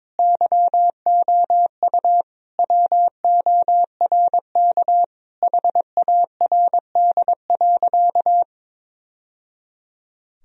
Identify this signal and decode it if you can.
Morse code, heard as a keyed tone, about 21 wpm, 700 Hz. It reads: YOU WORK HARD.
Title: Sentences from Top 1000 Words - Review Part 1 - 22wpm